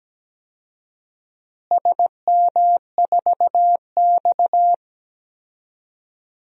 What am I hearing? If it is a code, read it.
Morse code, heard as a keyed tone, about 17 wpm, 700 Hz. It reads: SM4X
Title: DXCC-100 Call Signs - 4 Characters - Repeated 3x - 17wpm